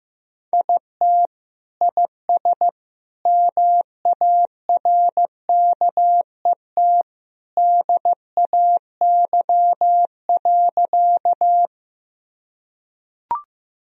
Morse code, heard as a keyed tone, about 15 wpm, 700 Hz. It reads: IT IS MARKET DAY.
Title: Sentences from Top 1000 Words - Review Part 3 - 15wpm